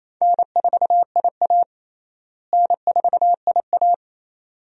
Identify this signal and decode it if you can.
Morse code, heard as a keyed tone, about 27 wpm, 700 Hz. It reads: N4IA N4IA
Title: USA Call Signs - 4 Characters - Repeated 2x - 28wpm